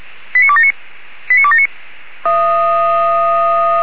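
Five-tone selective call (ZVEI), heard as quick runs of five tones, tones of about 70 ms. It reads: 89279 89279
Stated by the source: ZVEI five-tone call received by SDR.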